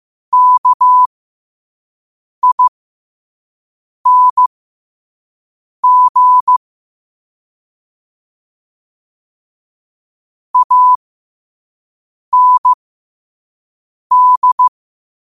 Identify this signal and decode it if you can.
Morse code, heard as a keyed tone, about 15 wpm, spaced as for 5 wpm, 1000 Hz. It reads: KING AND
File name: qotdfiles / 5wpm